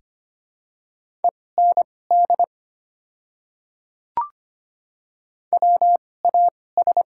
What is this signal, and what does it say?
Morse code, heard as a keyed tone, about 25 wpm, spaced as for 18 wpm, 700 Hz. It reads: END WAS
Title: 2x Character Spacing - Top 300 Words - 25wpm